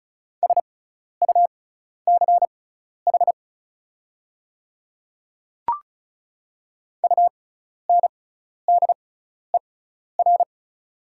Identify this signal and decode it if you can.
Morse code, heard as a keyed tone, about 36 wpm, spaced as for 12 wpm, 700 Hz. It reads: SUCH UNDER
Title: 6x Character Spacing - Top 300 Words - 35wpm